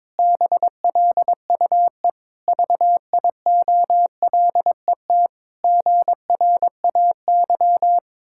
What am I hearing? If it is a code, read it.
Morse code, heard as a keyed tone, about 22 wpm, 700 Hz. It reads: BLUE VIOLET GRAY
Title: Pangram Challenge v2 - 22wpm